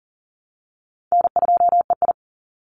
Morse code, heard as a keyed tone, about 40 wpm, 700 Hz. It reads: N2EI
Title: Easy Call Signs Repeated 2x - 40wpm